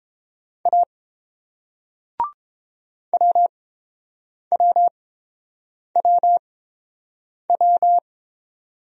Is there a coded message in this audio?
Yes: Morse code from 0.7 to 8.0 s, heard as a keyed tone, about 30 wpm, spaced as for 7 wpm, 700 Hz. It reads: A WWWW